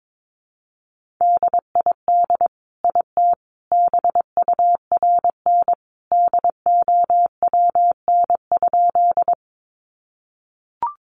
Morse code, heard as a keyed tone, about 22 wpm, 700 Hz. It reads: DID IT BURN DOWN?
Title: Sentences from Top 1300 Words 22wpm